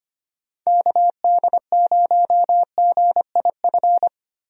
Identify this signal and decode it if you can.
Morse code, heard as a keyed tone, about 25 wpm, 700 Hz. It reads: KD0GIF
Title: USA Call Signs - 6 Characters - 25wpm